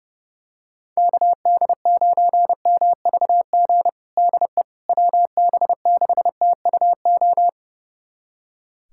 Morse code, heard as a keyed tone, about 30 wpm, 700 Hz. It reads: KD9MVG DE WB6TUO
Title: USA Callsign DE Callsign 30wpm